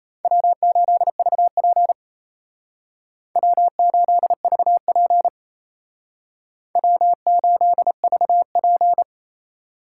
Morse code, heard as a keyed tone, about 35 wpm, 700 Hz. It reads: W8VP W8VP W8VP